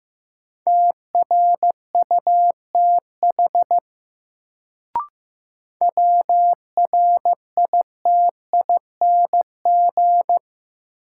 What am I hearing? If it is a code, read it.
Morse code, heard as a keyed tone, about 16 wpm, 700 Hz. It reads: TRUTH WRITING
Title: Sentences from Top 800 Words - New Word Drill - 15wpm